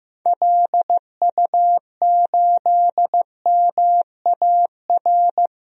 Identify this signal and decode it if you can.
Morse code, heard as a keyed tone, about 16 wpm, 700 Hz. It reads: LU8MAR
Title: DXCC-100 Call Signs - 6 Characters - Repeated 2x - 15wpm